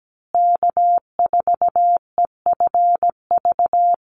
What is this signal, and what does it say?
Morse code, heard as a keyed tone, about 17 wpm, 700 Hz. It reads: K4EFV
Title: USA Call Signs Repeated 3x 17wpm